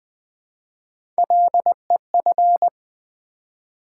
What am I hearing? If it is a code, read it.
Morse code, heard as a keyed tone, about 20 wpm, 700 Hz. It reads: LEF
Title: Build A Word - 1 Letter At A Time - 3 to 5-Letter Words - 20wpm